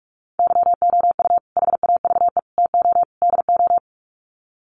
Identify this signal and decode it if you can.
Morse code, heard as a keyed tone, about 44 wpm, 700 Hz. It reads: YOU HAVE TO DO
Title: Sets of 4 Words 45wpm